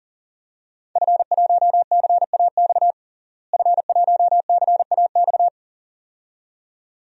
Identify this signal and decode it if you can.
Morse code, heard as a keyed tone, about 40 wpm, 700 Hz. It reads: F1CAX F1CAX
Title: DXCC-100 Call Signs - 5 Characters - Repeated 2x - 40wpm